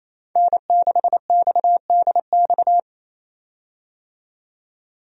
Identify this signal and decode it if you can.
Morse code, heard as a keyed tone, about 28 wpm, 700 Hz. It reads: N6XDX
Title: USA Call Signs - 5 Characters - 28wpm